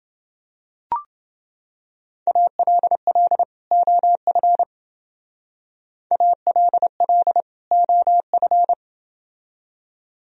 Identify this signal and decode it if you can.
Morse code, heard as a keyed tone, about 28 wpm, 700 Hz. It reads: ALL OF ALL OF